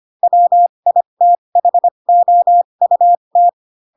Morse code, heard as a keyed tone, about 25 wpm, spaced as for 22 wpm, 700 Hz. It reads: WITHOUT
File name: Pinocchio / Pinocchio ch7 18 25 ik1hge